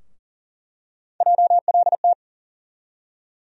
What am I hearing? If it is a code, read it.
Morse code, heard as a keyed tone, about 40 wpm, 700 Hz. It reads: JLT
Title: USA Call Sign Postfixes - 3 Letters 40wpm